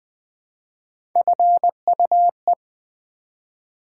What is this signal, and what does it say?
Morse code, heard as a keyed tone, about 20 wpm, 700 Hz. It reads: FUE
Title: Build A Word - 1 Letter At A Time - 4-Letter Words - 20wpm